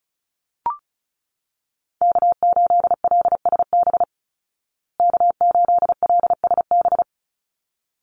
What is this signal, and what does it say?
Morse code, heard as a keyed tone, about 34 wpm, 700 Hz. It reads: K8LSB K8LSB
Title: Call Signs Repeated 3x - 35wpm